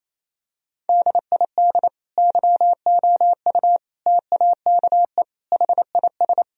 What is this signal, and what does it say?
Morse code, heard as a keyed tone, about 28 wpm, 700 Hz. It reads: DID YOU TAKE HIS
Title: Sentences from Top 1000 Words - Review Part 1 - 28wpm